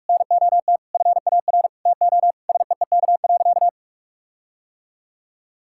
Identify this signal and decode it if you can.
Morse code, heard as a keyed tone, about 46 wpm, 700 Hz. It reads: NOT FAR TO SEEK.
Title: Sentences from Top 1600 Words 45wpm